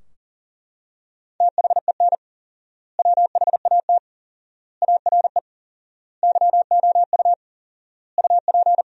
Morse code, heard as a keyed tone, about 40 wpm, 700 Hz. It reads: THEN WHAT ARE YOU UP
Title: 4x Word-Spacing - Sentences from Top 100 Words - 40wpm